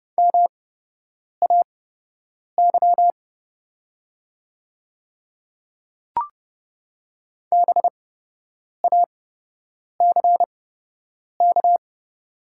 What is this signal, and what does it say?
Morse code, heard as a keyed tone, about 30 wpm, spaced as for 8 wpm, 700 Hz. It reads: MAY BACK
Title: 8x Character Spacing - Top 300 Words - 30wpm